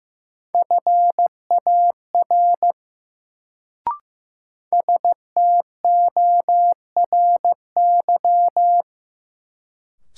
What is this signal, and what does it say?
Morse code, heard as a keyed tone, about 15 wpm, 700 Hz. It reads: FAR STORY
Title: Sentences from Top 300 Words - New Word Drill - 15wpm